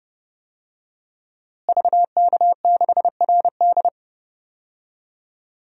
Morse code, heard as a keyed tone, about 32 wpm, 700 Hz. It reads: VK6RD
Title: DXCC-100 Call Signs - 5 Characters - Repeated 3x - 30wpm